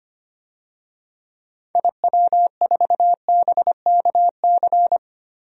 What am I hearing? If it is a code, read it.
Morse code, heard as a keyed tone, about 25 wpm, 700 Hz. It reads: IW4BKC
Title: DXCC-100 Call Signs - 6 Characters - Repeated 2x - 25wpm